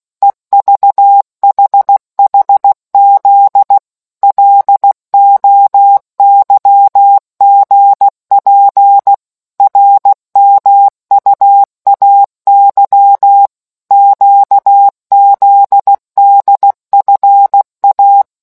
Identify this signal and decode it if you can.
Morse code, heard as a keyed tone, about 16 wpm, 800 Hz. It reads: EVHHZ LOYGP RMUAY QZDFA